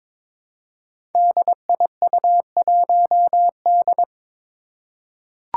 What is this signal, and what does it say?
Morse code, heard as a keyed tone, about 22 wpm, 700 Hz. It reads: DIU1D
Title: Five Challenging Letter-Number Combinations 22wpm